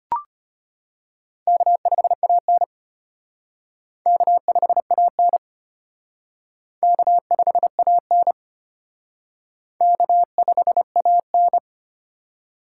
Morse code, heard as a keyed tone, about 31 wpm, 700 Hz. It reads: K5AN K5AN K5AN K5AN
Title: Speed Racing - USA 1x2 Call Signs 25wpm